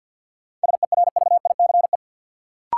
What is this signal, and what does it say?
Morse code, heard as a keyed tone, about 52 wpm, 700 Hz. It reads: SERVICE
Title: Sentences from Top 700 Words - New Word Drill - 50wpm